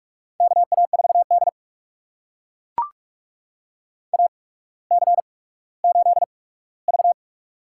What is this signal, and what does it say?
Morse code, heard as a keyed tone, about 44 wpm, spaced as for 19 wpm, 690 Hz. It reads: KA4D AC8V
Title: USA Call Signs - 4 Characters - 8x Character Spacing - 45wpm